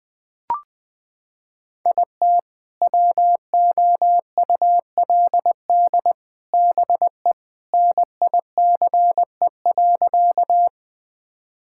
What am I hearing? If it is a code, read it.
Morse code, heard as a keyed tone, about 20 wpm, 700 Hz. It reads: IT WOULD BE NICE.
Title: Sentences with 3-Word Phrases - 20wpm